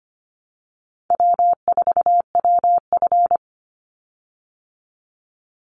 Morse code, heard as a keyed tone, about 25 wpm, 700 Hz. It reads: W4WF
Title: Easy Call Signs Repeated 2x - 25wpm